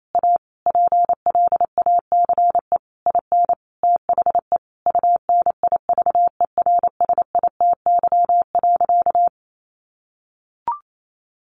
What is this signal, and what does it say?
Morse code, heard as a keyed tone, about 28 wpm, 700 Hz. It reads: A PLACE IN THE UNIVERSITY.
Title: Sentences from Top 1300 Words 28wpm